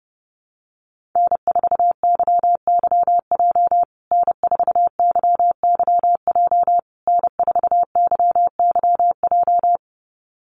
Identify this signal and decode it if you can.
Morse code, heard as a keyed tone, about 30 wpm, 700 Hz. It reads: N4YYJ N4YYJ N4YYJ
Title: USA Call Signs Repeated 3x 30wpm